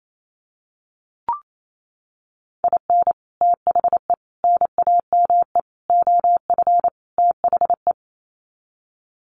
Morse code, heard as a keyed tone, about 28 wpm, 700 Hz. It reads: IN THE NAME OF THE
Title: Sets of 5 Words Encore 28wpm